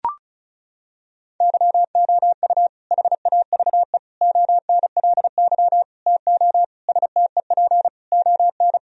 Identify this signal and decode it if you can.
Morse code, heard as a keyed tone, about 36 wpm, 700 Hz. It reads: YOU HAVE ONLY TO STEP ON
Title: Sentences from Top 1000 Words - Review Part 4 - 35wpm